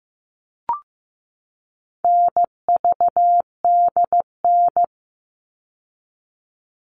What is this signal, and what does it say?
Morse code, heard as a keyed tone, about 15 wpm, 700 Hz. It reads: NVDN
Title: All Cut Numbers - 4-Digit 15wpm